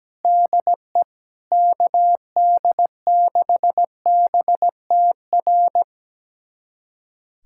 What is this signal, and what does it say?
Morse code, heard as a keyed tone, about 17 wpm, 700 Hz. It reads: DE KD6BTR
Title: USA Callsign DE Callsign 17wpm